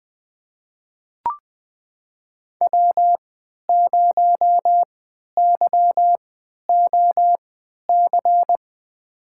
Morse code, heard as a keyed tone, about 21 wpm, spaced as for 12 wpm, 700 Hz. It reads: W0YOC
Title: USA Call Signs - 5 Characters - 3x Character Spacing - 20wpm